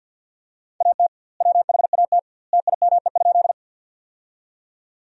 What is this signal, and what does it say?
Morse code, heard as a keyed tone, about 50 wpm, 700 Hz. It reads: AT WHAT TIME?
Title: 2x Word-Spacing - Sentences from Top 100 Words - 50wpm